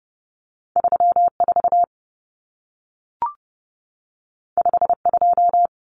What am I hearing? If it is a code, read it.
Morse code, heard as a keyed tone, about 30 wpm, 700 Hz. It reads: 34 52
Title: Numbers - 2-Digit 30wpm